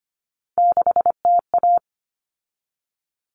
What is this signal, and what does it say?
Morse code, heard as a keyed tone, about 25 wpm, 700 Hz. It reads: 6TA
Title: Common Cut Numbers - 3-Digit 25wpm